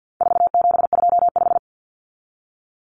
Morse code, heard as a keyed tone, about 50 wpm, 700 Hz. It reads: 4725
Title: Numbers - 4-Digit 50wpm